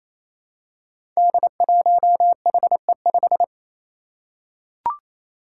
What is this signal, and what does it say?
Morse code, heard as a keyed tone, about 27 wpm, 700 Hz. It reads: D1HE5
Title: Five Challenging Letter-Number Combinations 28wpm